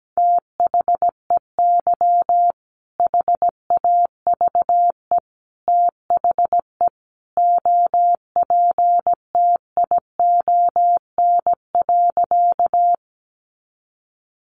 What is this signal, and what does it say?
Morse code, heard as a keyed tone, about 17 wpm, 700 Hz. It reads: THEY HAVE THE OPTION.
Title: Sentences from Top 1300 Words 17wpm